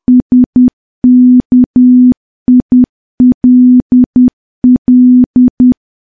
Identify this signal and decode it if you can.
Morse code, heard as a keyed tone, about 10 wpm, 250 Hz. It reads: SKILL
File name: morse / morse-training-10WPM